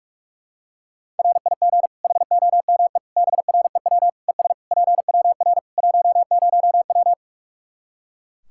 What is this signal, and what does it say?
Morse code, heard as a keyed tone, about 43 wpm, 700 Hz. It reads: RIG HOME BREW ES PWR 10W
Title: Practice QSO - Exchange 2 - Info - 45wpm